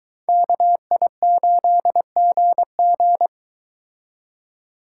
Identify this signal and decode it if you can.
Morse code, heard as a keyed tone, about 23 wpm, 700 Hz. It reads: KI8GG